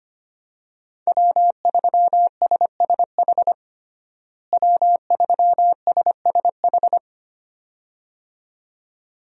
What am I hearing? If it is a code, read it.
Morse code, heard as a keyed tone, about 25 wpm, 700 Hz. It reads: W3SSH W3SSH
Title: USA Call Signs - 5 Characters - Repeated 2x - 25wpm